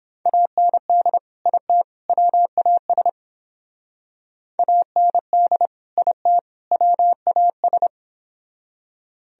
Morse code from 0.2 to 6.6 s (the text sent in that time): AND IT WAS AND IT